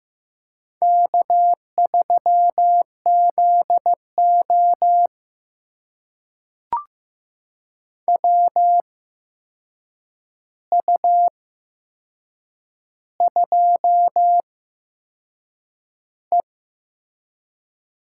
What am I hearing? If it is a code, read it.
Morse code, heard as a keyed tone, about 15 wpm, spaced as for 6 wpm, 700 Hz. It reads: K3ZO WU2E
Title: USA Call Signs - 4 Characters - 8x Character Spacing - 15wpm